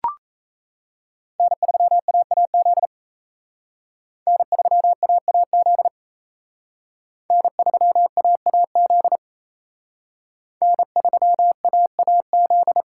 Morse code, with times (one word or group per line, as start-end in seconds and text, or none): 1.4-2.9 N3AAZ
4.3-5.9 N3AAZ
7.3-9.2 N3AAZ
10.6-12.8 N3AAZ